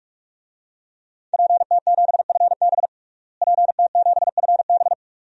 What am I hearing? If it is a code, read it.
Morse code, heard as a keyed tone, about 41 wpm, 700 Hz. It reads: PT7FB PT7FB
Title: DXCC-100 Call Signs - 5 Characters - Repeated 2x - 45wpm